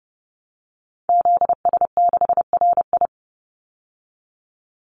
Morse code, heard as a keyed tone, about 30 wpm, 700 Hz. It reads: ZS6RI
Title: Call Signs - 30wpm